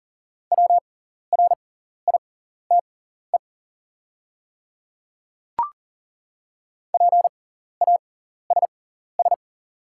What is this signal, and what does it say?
Morse code, heard as a keyed tone, about 40 wpm, spaced as for 14 wpm, 700 Hz. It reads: WRITE PASS
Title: 6x Character Spacing - Top 500 Words - 40wpm